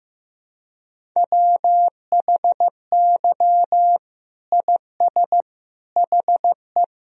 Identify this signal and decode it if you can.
Morse code, heard as a keyed tone, about 15 wpm, 700 Hz. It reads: WHY IS HE
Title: Sentences from Top 300 Words and Abbreviations 15wpm